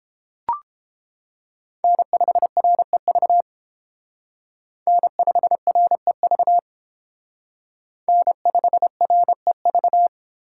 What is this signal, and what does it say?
Morse code, heard as a keyed tone, about 30 wpm, 700 Hz. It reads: N5REV N5REV N5REV